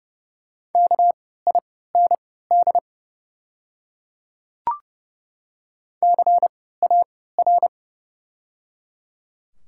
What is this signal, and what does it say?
Morse code, heard as a keyed tone, about 30 wpm, spaced as for 17 wpm, 700 Hz. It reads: KIND CAR
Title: 3x Character Spacing - Top 300 Words - 30wpm